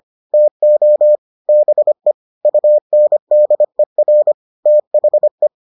Morse code, heard as a keyed tone, about 25 wpm, 600 Hz. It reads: TO BE UNDER THE